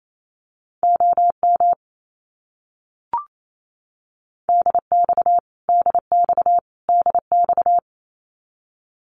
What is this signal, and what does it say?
Morse code, heard as a keyed tone, about 28 wpm, 700 Hz. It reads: OM DX DX DX
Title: Abbreviations Repeated 3x - 28wpm